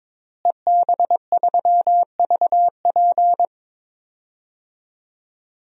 Morse code, heard as a keyed tone, about 22 wpm, 700 Hz. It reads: EB3VP
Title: DXCC-100 Call Signs - 5 Characters - Repeated 3x - 22wpm